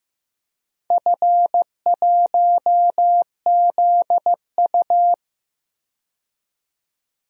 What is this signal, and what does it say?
Morse code, heard as a keyed tone, about 15 wpm, 700 Hz. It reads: F1ZU
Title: DXCC-100 Call Signs - 4 Characters - 15wpm